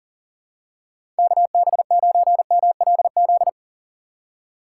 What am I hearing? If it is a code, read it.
Morse code, heard as a keyed tone, about 40 wpm, 700 Hz. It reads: KB9MLZ
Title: USA Call Signs - 6 Characters - 6x Character Spacing - 40wpm